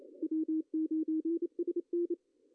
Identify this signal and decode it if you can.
Morse code, heard as a keyed tone, about 29 wpm, 310 Hz. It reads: W9SN
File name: CW / cqww-cw-k5tr